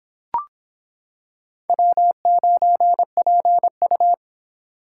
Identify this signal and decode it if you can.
Morse code, heard as a keyed tone, about 26 wpm, 700 Hz. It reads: W9PU